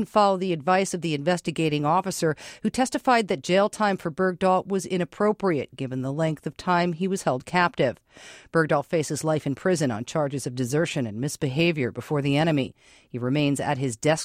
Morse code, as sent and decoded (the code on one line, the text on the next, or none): none